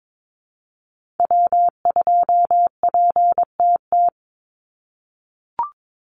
Morse code, heard as a keyed tone, about 22 wpm, 700 Hz. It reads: W2PTT
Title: Call Signs - 22wpm